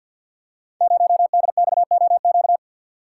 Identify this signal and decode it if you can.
Morse code, heard as a keyed tone, about 51 wpm, 700 Hz. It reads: 0DXOQ
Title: Code Groups - 5 Letter:Numbers - 50wpm